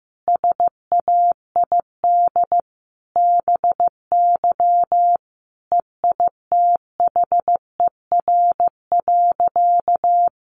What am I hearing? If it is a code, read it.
Morse code, heard as a keyed tone, about 15 wpm, 700 Hz. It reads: SAID BY EITHER.
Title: Sentences from Top 600 Words 15wpm